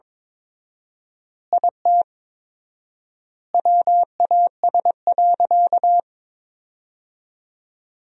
Morse code, heard as a keyed tone, about 22 wpm, 700 Hz. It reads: IT WAS.
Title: 4x Word-Spacing - Sentences from Top 100 Words - 22wpm